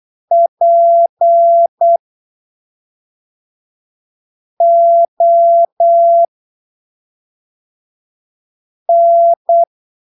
Morse code, heard as a keyed tone, about 8 wpm, spaced as for 3 wpm, 680 Hz. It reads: PON
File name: arquivos / texto / texto 28 08ppm